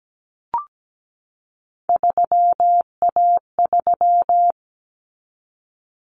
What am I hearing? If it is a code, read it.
Morse code, heard as a keyed tone, about 17 wpm, 700 Hz. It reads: 3A3
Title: Common Cut Numbers - 3-Digit 17wpm